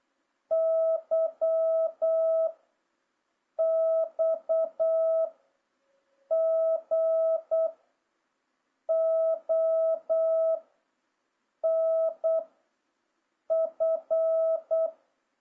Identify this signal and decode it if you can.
Morse code, heard as a keyed tone, about 8 wpm, 650 Hz. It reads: Y X G O N F
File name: jh8cbh / 2026cw